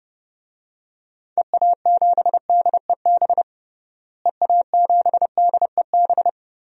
Morse code, heard as a keyed tone, about 32 wpm, 700 Hz. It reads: EA7DEB EA7DEB
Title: DXCC-100 Call Signs - 6 Characters - Repeated 2x - 30wpm